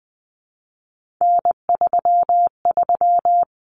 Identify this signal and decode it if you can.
Morse code, heard as a keyed tone, about 20 wpm, 700 Hz. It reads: N33
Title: Common Cut Numbers - 3-Digit 20wpm